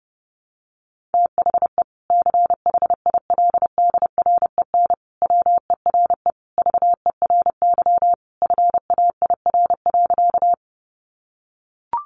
Morse code, heard as a keyed tone, about 28 wpm, 700 Hz. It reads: THE CHILDREN WERE VERY FAIR.
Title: Sentences from Top 1300 Words 30wpm